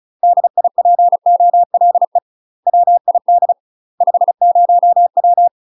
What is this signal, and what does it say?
Morse code, heard as a keyed tone, about 36 wpm, 700 Hz. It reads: DIPOLE WID 50W
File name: UK MASTER QSOs TRAINING / 35WPM / UK MASTER QSOs 0009